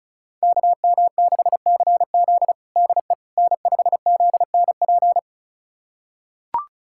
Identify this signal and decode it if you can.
Morse code, heard as a keyed tone, about 36 wpm, 700 Hz. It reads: KM6CZ DE N5ZNP
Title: USA Callsign DE Callsign - 5 Characters - 35wpm